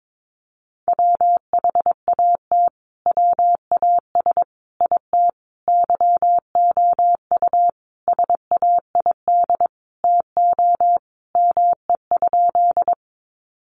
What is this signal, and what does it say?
Morse code, heard as a keyed tone, about 22 wpm, 700 Hz. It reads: WHAT WAS IT YOU SAID TO ME?
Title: Sentences from Top 200 Words 22wpm